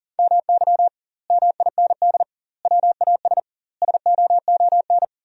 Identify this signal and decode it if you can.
Morse code, heard as a keyed tone, about 40 wpm, 700 Hz. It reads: MY MIND WAS SOON
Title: 2x Word-Spacing - Sentences from Top 500 Words - 40wpm